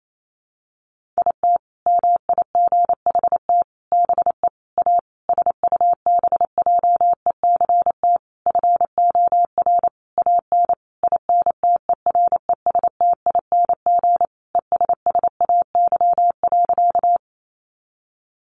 Morse code, heard as a keyed tone, about 28 wpm, 700 Hz. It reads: IT MIGHT BE A SUBJECT FOR AN INTERESTING ESSAY.
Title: Sentences from Top 1500 Words 28wpm